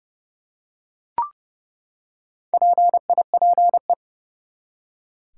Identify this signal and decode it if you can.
Morse code, heard as a keyed tone, about 31 wpm, 700 Hz. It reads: PIPE
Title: Four Letter Words - 30wpm